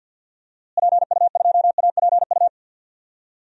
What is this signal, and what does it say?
Morse code, heard as a keyed tone, about 46 wpm, 700 Hz. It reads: PU2APU
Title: DXCC-100 Call Signs - 6 Characters - Repeated 2x - 50wpm